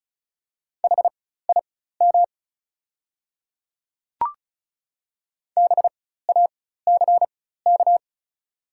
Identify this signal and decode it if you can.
Morse code, heard as a keyed tone, about 35 wpm, spaced as for 16 wpm, 700 Hz. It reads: HIM BACK
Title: 4x Character Spacing - Top 300 Words - 35wpm